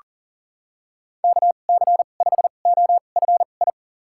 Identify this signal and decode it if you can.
Morse code, heard as a keyed tone, about 40 wpm, spaced as for 29 wpm, 700 Hz. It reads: KC5OFI
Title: USA Call Signs - 6 Characters - 2x Character Spacing - 40wpm